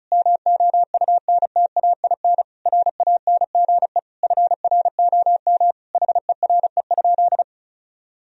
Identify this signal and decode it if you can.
Morse code, heard as a keyed tone, about 34 wpm, 700 Hz. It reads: MOUNTAIN RANGE FROM HERE?
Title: Sentences from Top 1000 Words - Review Part 4 - 35wpm